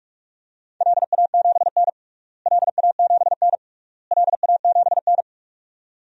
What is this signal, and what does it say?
Morse code, heard as a keyed tone, about 46 wpm, 690 Hz. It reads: LA7N LA7N LA7N
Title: DXCC-100 Call Signs - 4 Characters - Repeated 3x - 45wpm